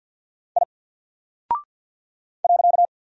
Morse code, heard as a keyed tone, about 48 wpm, 700 Hz. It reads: I .
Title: Single Character (Letter:Number:Punctuation:Prosign) Alt 50wpm